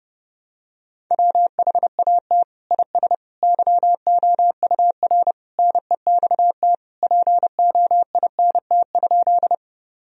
Morse code, heard as a keyed tone, about 30 wpm, 700 Hz. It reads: WHAT IS YOUR NEXT POINT?
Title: Sentences from Top 1000 Words - Review Part 1 - 30wpm